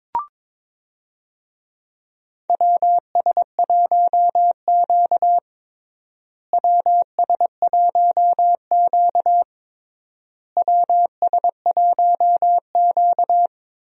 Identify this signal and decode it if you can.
Morse code, heard as a keyed tone, about 22 wpm, 700 Hz. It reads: WS1Q WS1Q WS1Q
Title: USA Call Signs - 4 Characters - Repeated 3x - 22wpm